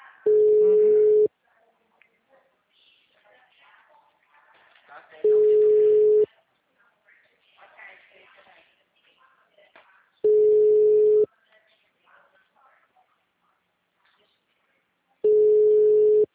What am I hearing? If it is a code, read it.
Ringback tone (Europe, 425 Hz), with speech.